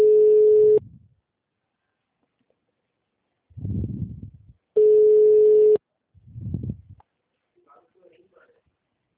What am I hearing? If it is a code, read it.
Ringback tone (Europe, 425 Hz), with speech.